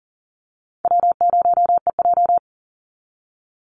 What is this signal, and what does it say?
Morse code, heard as a keyed tone, about 40 wpm, 700 Hz. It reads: W0EJ
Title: Easy Call Signs Repeated 2x - 40wpm